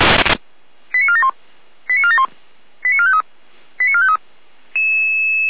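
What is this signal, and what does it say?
Five-tone selective call (ZVEI), heard as quick runs of five tones, tones of about 70 ms. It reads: 89471 89471 89453 89453
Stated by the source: Five-tone ZVEI selective call 89325 received by SDR.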